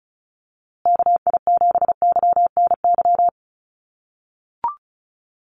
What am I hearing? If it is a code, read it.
Morse code, heard as a keyed tone, about 34 wpm, 700 Hz. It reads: KI7YNY
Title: USA Call Signs Repeated 3x 35wpm